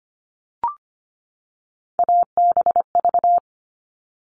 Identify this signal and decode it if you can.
Morse code, heard as a keyed tone, about 25 wpm, 700 Hz. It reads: ABV
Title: All Cut Numbers - 3-Digit 25wpm